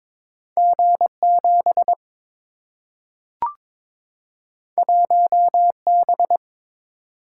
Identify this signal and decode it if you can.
Morse code, heard as a keyed tone, about 22 wpm, 700 Hz. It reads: G 7 1 B
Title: Single-Letter Single-Number 22wpm